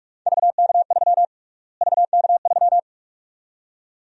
Morse code, heard as a keyed tone, about 46 wpm, 700 Hz. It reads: VK3 VK3
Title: USA Call Sign Prefixes - Repeated 2x - 45wpm